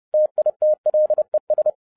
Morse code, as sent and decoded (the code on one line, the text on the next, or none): - .. - .-.. ...
TITLES